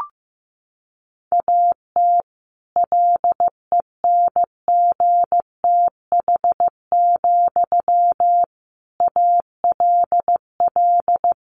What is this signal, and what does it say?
Morse code, heard as a keyed tone, about 15 wpm, 700 Hz. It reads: AT LENGTH, ALL
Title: Sentences from Top 1000 Words 15wpm